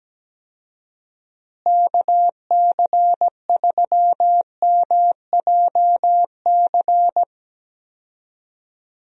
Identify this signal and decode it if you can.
Morse code, heard as a keyed tone, about 17 wpm, 700 Hz. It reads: KC3MJC
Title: USA Call Signs - 6 Characters - Repeated 3x - 17wpm